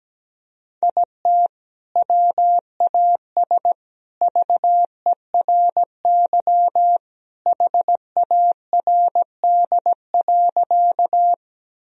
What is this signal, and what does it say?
Morse code, heard as a keyed tone, about 17 wpm, 700 Hz. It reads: IT WAS VERY HARD.
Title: Sentences from Top 1000 Words - Review Part 1 - 17wpm